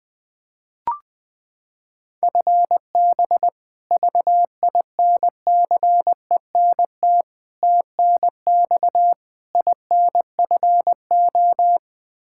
Morse code, heard as a keyed tone, about 20 wpm, 700 Hz. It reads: FB VINCENT TNX INFO
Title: Practice QSO - Exchange 3 - More Info - 20wpm